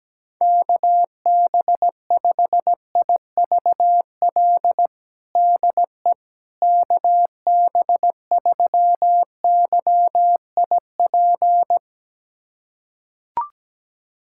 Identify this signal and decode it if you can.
Morse code, heard as a keyed tone, about 17 wpm, 700 Hz. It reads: KB5IVL DE KB3YIP
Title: USA Callsign DE Callsign - 6 Characters - 17wpm